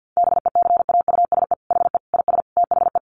Morse code, heard as a keyed tone, about 50 wpm, 700 Hz. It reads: BECAUSE HE IS THE
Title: Sentences from Top 1500 Words 50wpm